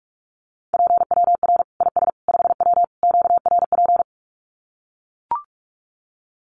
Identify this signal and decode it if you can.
Morse code, heard as a keyed tone, about 44 wpm, 700 Hz. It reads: PWR IS 5W QRP
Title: Common QSO Phrases 45wpm